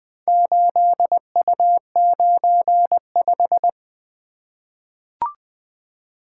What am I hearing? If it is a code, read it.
Morse code, heard as a keyed tone, about 20 wpm, 700 Hz. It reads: 8U95
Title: Four Letter-Number Combinations - 20wpm